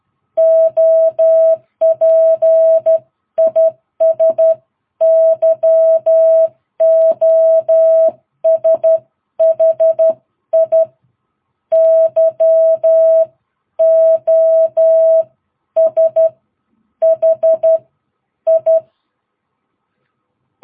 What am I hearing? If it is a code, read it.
Morse code, heard as a keyed tone, about 12 wpm, spaced as for 10 wpm, 660 Hz. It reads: OP IS YOSHI YOSHI
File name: jh8cbh / 2022cw